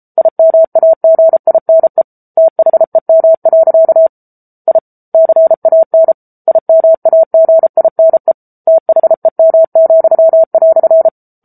Morse code, heard as a keyed tone, about 34 wpm, 650 Hz. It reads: IMAGINE THEM. I CAN IMAGINE THEM,"
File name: Book Two / 07-Chapter 7 - 33 WPM